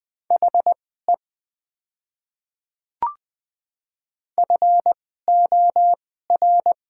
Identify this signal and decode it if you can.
Morse code, heard as a keyed tone, about 20 wpm, spaced as for 15 wpm, 700 Hz. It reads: HE FOR
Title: 2x Character Spacing - Top 100 Words - 20wpm